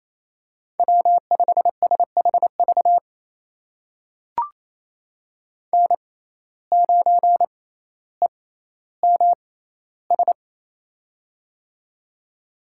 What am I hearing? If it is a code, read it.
Morse code, heard as a keyed tone, about 28 wpm, spaced as for 14 wpm, 700 Hz. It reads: W5SHV N9EMS
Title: USA Call Signs - 5 Characters - 6x Character Spacing - 28wpm